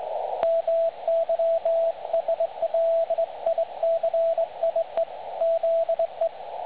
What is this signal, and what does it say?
Morse code, heard as a keyed tone, about 20 wpm, 660 Hz, with radio noise. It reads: MYSLICIE ZE
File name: dlazaawansowanych / nagraniazpasma / websdr recording 3565.0kHz